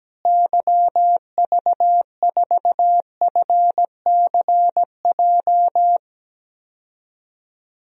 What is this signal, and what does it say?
Morse code, heard as a keyed tone, about 17 wpm, 700 Hz. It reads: YV4FCJ
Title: DXCC-100 Call Signs - 6 Characters - Repeated 3x - 17wpm